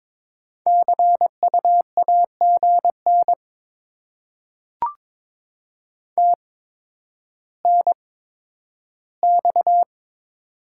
Morse code, heard as a keyed tone, about 21 wpm, spaced as for 11 wpm, 700 Hz. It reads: CU AGN TNX
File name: Top 20 QSO Elements - 8x Character Spacing - 22wpm